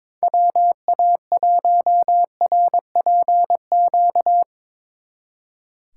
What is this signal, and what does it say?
Morse code, heard as a keyed tone, about 24 wpm, 700 Hz. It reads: WA1RPQ
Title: Speed Racing - USA 2x3 Call Signs 22wpm